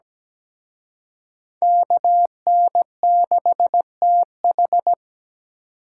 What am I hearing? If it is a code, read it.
Morse code, heard as a keyed tone, about 20 wpm, 700 Hz. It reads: KN6TH
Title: Speed Racing - USA 2x2 Call Signs 17wpm